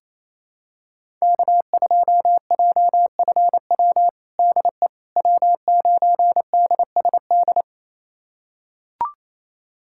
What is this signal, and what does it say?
Morse code, heard as a keyed tone, about 27 wpm, 700 Hz. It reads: K2JFW DE W9DSD
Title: USA Callsign DE Callsign - 5 Characters - 28wpm